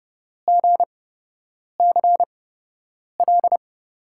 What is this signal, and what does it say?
Morse code, heard as a keyed tone, about 30 wpm, spaced as for 8 wpm, 700 Hz. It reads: GCL
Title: USA Call Signs - 5 Characters - 8x Character Spacing - 30wpm